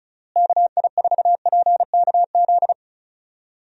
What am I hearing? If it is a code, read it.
Morse code, heard as a keyed tone, about 34 wpm, 700 Hz. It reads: KI4PKZ
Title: USA Call Signs - 6 Characters - Repeated 3x - 35wpm